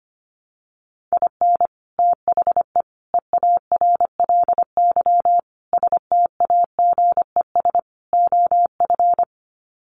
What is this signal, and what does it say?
Morse code, heard as a keyed tone, about 25 wpm, 700 Hz. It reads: IN THE EARLY STAGES OF
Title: Sets of 5 Words Encore 25wpm